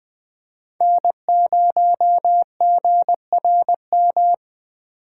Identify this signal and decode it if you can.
Morse code, heard as a keyed tone, about 20 wpm, 700 Hz. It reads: N0GRM
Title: Speed Racing - USA 1x3 Call Signs 17wpm